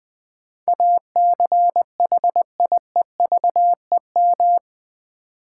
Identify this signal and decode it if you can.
Morse code, heard as a keyed tone, about 20 wpm, 700 Hz. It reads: ACHIEVEM
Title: Build A Word - 4 Letters At A Time - 11-Letter Words - 20wpm